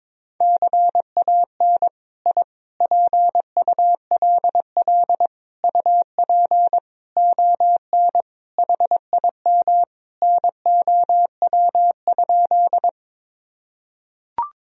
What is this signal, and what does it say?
Morse code, heard as a keyed tone, about 21 wpm, 700 Hz. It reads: CAN I PULL UP ON HIM NOW?
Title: Sentences from Top 1000 Words - Review Part 2 - 22wpm